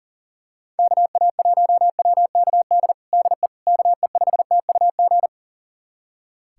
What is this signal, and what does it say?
Morse code, heard as a keyed tone, about 40 wpm, 700 Hz. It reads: KA1WKD DE KE5TUG
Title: USA Callsign DE Callsign - 6 Characters - 40wpm